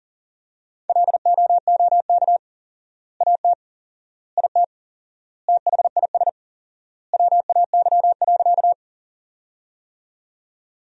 Morse code, heard as a keyed tone, about 40 wpm, 700 Hz. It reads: LOOK AT IT THIS WAY.
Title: 4x Word-Spacing - Sentences from Top 100 Words - 40wpm